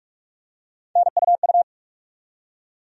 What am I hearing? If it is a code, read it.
Morse code, heard as a keyed tone, about 46 wpm, 700 Hz. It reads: NUU
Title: USA Call Sign Postfixes - 3 Letters 45wpm